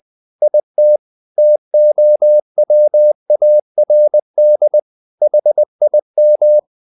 Morse code, heard as a keyed tone, about 20 wpm, 600 Hz. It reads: IT TOWARD HIM
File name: the old man and the sea / the old man and the sea part 1 20wpm